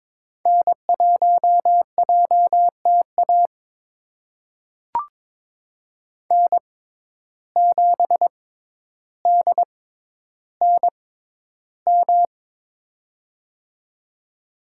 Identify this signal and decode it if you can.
Morse code, heard as a keyed tone, about 22 wpm, spaced as for 11 wpm, 700 Hz. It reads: N1JTA N7DNM
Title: USA Call Signs - 5 Characters - 6x Character Spacing - 22wpm